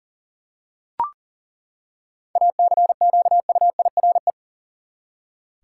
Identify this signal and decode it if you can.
Morse code, heard as a keyed tone, about 40 wpm, 700 Hz. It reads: ACQUIRE
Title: Seven Letter Words - 40wpm